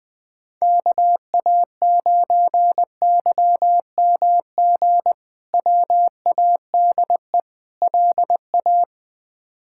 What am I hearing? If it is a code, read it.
Morse code, heard as a keyed tone, about 20 wpm, 700 Hz. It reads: KA9YMG WADE LA
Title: K1USN SST QSO Practice - 20wpm